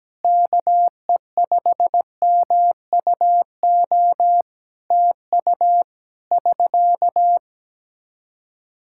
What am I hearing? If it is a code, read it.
Morse code, heard as a keyed tone, about 17 wpm, 700 Hz. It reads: KE5MUO TU <SK>
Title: K1USN SST QSO Practice 17wpm